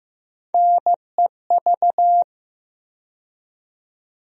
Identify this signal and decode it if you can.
Morse code, heard as a keyed tone, about 15 wpm, 700 Hz. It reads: NEV
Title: USA Call Sign Postfixes - 3 Letters 15wpm